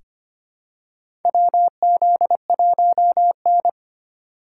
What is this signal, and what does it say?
Morse code, heard as a keyed tone, about 25 wpm, 700 Hz. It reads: WZ1N
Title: USA Call Signs - 3x Character Spacing - 25wpm